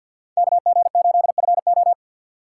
Morse code, heard as a keyed tone, about 52 wpm, 700 Hz. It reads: KG8FO
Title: USA Call Signs - 5 Characters - 2x Character Spacing - 50wpm